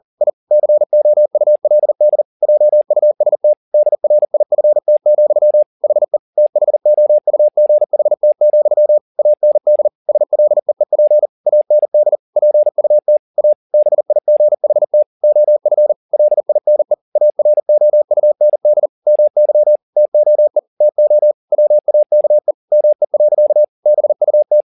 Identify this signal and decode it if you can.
Morse code, heard as a keyed tone, about 40 wpm, 600 Hz. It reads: I COULD JUST DRIFT, HE THOUGHT, AND SLEEP AND PUT A BIGHT OF LINE AROUND MY TOE TO WAKE ME. BUT